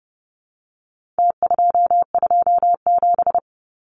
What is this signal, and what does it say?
Morse code, heard as a keyed tone, about 30 wpm, 700 Hz. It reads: T227
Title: Numbers - Contest Style - 4-Digit 30wpm